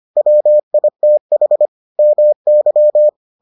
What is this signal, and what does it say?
Morse code, heard as a keyed tone, about 25 wpm, 600 Hz. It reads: WITH MY